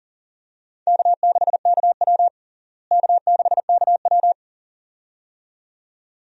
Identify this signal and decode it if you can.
Morse code, heard as a keyed tone, about 40 wpm, 700 Hz. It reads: K6KW K6KW
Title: USA Call Signs - 4 Characters - Repeated 2x - 40wpm